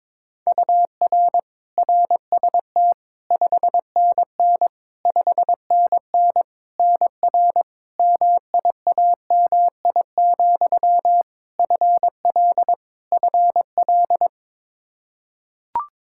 Morse code, heard as a keyed tone, about 22 wpm, 700 Hz. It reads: UR RST 5NN 5NN NR MIAMI, FL FL
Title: K1USN SST QSO Practice 22wpm